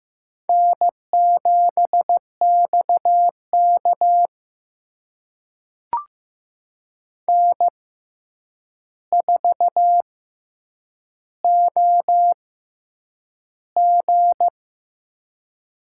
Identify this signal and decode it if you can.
Morse code, heard as a keyed tone, about 15 wpm, spaced as for 8 wpm, 700 Hz. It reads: N7XK N4OG
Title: USA Call Signs - 6x Character Spacing - 15wpm